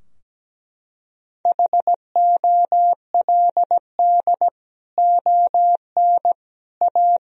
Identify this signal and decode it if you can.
Morse code, heard as a keyed tone, about 17 wpm, 700 Hz. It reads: HOLD ON A
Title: Sentences from Top 1000 Words - Review Part 2 - 17wpm